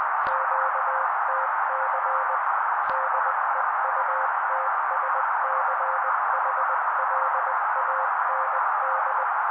Morse code, heard as a keyed tone, about 20 wpm, 530 Hz. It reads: QTC DEUTSCHLAND
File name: daily / 18wpm noise